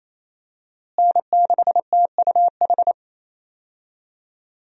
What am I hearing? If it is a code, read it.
Morse code, heard as a keyed tone, about 28 wpm, 700 Hz. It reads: N6TUH
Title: USA Call Signs - 5 Characters - Repeated 3x - 28wpm